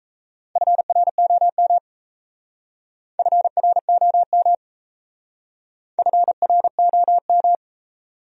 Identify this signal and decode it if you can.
Morse code, heard as a keyed tone, about 35 wpm, 700 Hz. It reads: FROM FROM FROM
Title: Speed-Racing Top 200 Words 28wpm